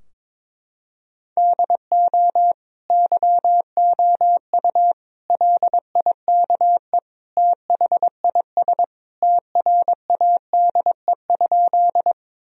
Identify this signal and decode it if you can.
Morse code, heard as a keyed tone, about 22 wpm, 700 Hz. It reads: DO YOU LIKE THIS TRADE?
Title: Sentences from Top 1000 Words - Review Part 4 - 22wpm